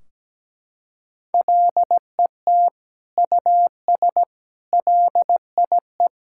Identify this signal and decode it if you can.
Morse code, heard as a keyed tone, about 17 wpm, 700 Hz. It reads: LET US LIE
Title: Sentences from Top 1600 Words 17wpm